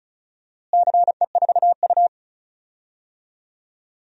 Morse code, heard as a keyed tone, about 36 wpm, 700 Hz. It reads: CE4U
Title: DXCC-100 Call Signs - 4 Characters - Repeated 2x - 35wpm